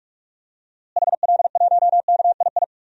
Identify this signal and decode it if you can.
Morse code, heard as a keyed tone, about 41 wpm, 700 Hz. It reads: HL1KII
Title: DXCC-100 Call Signs - 6 Characters - Repeated 2x - 45wpm